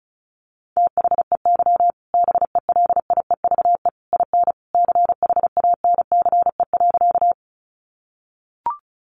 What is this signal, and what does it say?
Morse code, heard as a keyed tone, about 34 wpm, 700 Hz. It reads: THEY BELIEVE IN CHANCE.
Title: Sentences from Top 1000 Words 35wpm